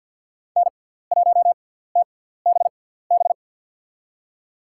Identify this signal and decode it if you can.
Morse code, heard as a keyed tone, about 47 wpm, spaced as for 17 wpm, 690 Hz. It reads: N1TBB
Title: USA Call Signs - 5 Characters - 6x Character Spacing - 50wpm